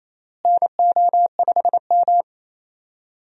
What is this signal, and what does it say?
Morse code, heard as a keyed tone, about 27 wpm, 700 Hz. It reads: NO5M
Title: USA Call Signs - 4 Characters - Repeated 3x - 28wpm